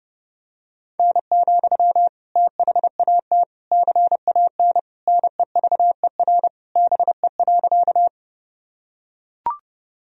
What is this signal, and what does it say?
Morse code, heard as a keyed tone, about 30 wpm, 700 Hz. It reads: N, THAT CAN NEVER BE.
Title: Sentences from Top 300 Words and Abbreviations 30wpm